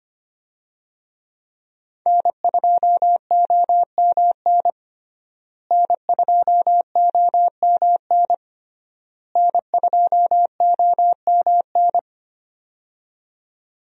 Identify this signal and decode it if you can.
Morse code, heard as a keyed tone, about 25 wpm, 700 Hz. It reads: N2OMN N2OMN N2OMN
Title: USA Call Signs - 5 Characters - Repeated 3x - 25wpm